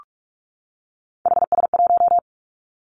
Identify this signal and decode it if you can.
Morse code, heard as a keyed tone, about 46 wpm, 700 Hz. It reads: HS1
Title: Three Challenging Letter-Number Combinations 45wpm